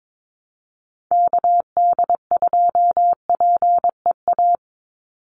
Morse code, heard as a keyed tone, about 22 wpm, 700 Hz. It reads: KD2PEA
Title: USA Call Signs 22wpm